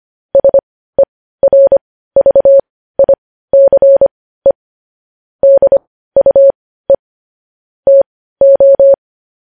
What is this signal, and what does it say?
Morse code, heard as a keyed tone, about 25 wpm, spaced as for 15 wpm, 550 Hz. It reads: SERVICE DUE TO
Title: News Headlines at 15